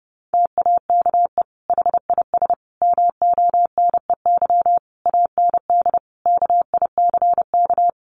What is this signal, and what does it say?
Morse code, heard as a keyed tone, about 30 wpm, 700 Hz. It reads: TAKE HIS MONEY AND KICK
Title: Sentences from Top 1300 Words 30wpm